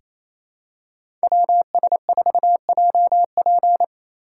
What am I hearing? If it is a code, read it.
Morse code, heard as a keyed tone, about 28 wpm, 700 Hz. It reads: WS4JP
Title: USA Call Signs - 5 Characters - 8x Character Spacing - 28wpm